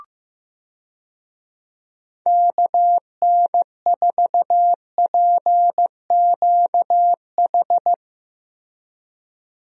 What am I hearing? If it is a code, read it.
Morse code, heard as a keyed tone, about 15 wpm, 700 Hz. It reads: KN4PQH
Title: USA Call Signs - 6 Characters - Repeated 2x - 15wpm